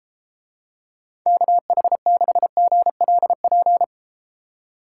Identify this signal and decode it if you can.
Morse code, heard as a keyed tone, about 33 wpm, 700 Hz. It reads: KH6GLP